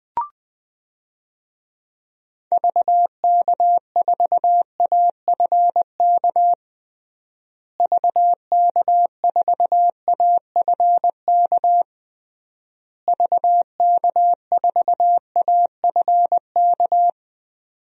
Morse code, heard as a keyed tone, about 20 wpm, 700 Hz. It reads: VK4AFK VK4AFK VK4AFK
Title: DXCC-100 Call Signs - 6 Characters - Repeated 3x - 20wpm